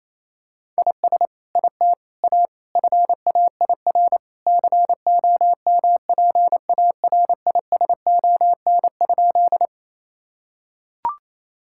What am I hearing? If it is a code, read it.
Morse code, heard as a keyed tone, about 28 wpm, 700 Hz. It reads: IS IT A FAIR COMPARISON?
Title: Sentences from Top 1600 Words 28wpm